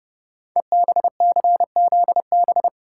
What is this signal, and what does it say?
Morse code, heard as a keyed tone, about 28 wpm, 700 Hz. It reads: EBCZB
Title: Code Groups - 5 Letters - 30wpm